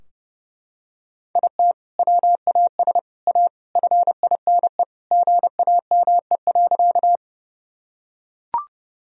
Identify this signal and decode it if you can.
Morse code, heard as a keyed tone, about 32 wpm, 700 Hz. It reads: IT WAS A FINE GAME.
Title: Sentences from Top 1000 Words - Review Part 3 - 30wpm